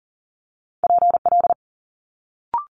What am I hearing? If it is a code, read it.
Morse code, heard as a keyed tone, about 41 wpm, 700 Hz. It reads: PL
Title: Multi-letter Phonemes Quick Fire 40wpm